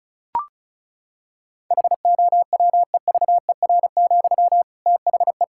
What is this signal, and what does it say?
Morse code, heard as a keyed tone, about 34 wpm, 700 Hz. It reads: HOWEVER, THE
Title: Sentences from Top 1600 Words 35wpm